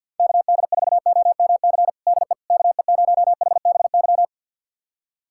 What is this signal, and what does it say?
Morse code, heard as a keyed tone, about 46 wpm, 700 Hz. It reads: KD4OMC DE KE0HBY
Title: USA Callsign DE Callsign - 6 Characters - 50wpm